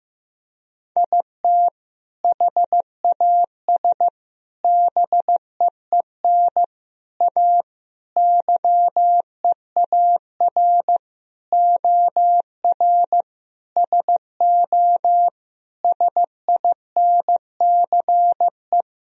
Morse code, heard as a keyed tone, about 15 wpm, 700 Hz. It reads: IT HAS BEEN A YEAR OR SO SINCE
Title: Sentences from Top 1000 Words - Review Part 1 - 15wpm